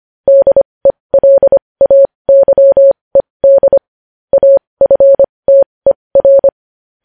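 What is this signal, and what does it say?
Morse code, heard as a keyed tone, about 25 wpm, spaced as for 20 wpm, 550 Hz. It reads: DELAYED AFTER
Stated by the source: Morse code news headlines practice at 20 wpm.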